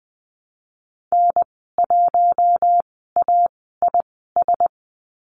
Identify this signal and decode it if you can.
Morse code, heard as a keyed tone, about 20 wpm, spaced as for 15 wpm, 700 Hz. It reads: N1AIS
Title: 048 Lesson 9g - Number 4 - Call Signs 20wpm